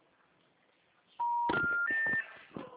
Special information tone.
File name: out- -2003- - - .62